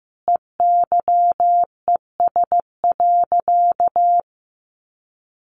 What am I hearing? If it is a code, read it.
Morse code, heard as a keyed tone, about 15 wpm, 700 Hz. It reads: EYES.
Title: Easy Phrases 15wpm